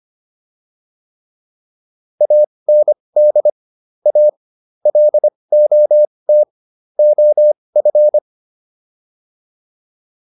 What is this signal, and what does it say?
Morse code, heard as a keyed tone, about 25 wpm, spaced as for 20 wpm, 600 Hz. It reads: AND A LOT OF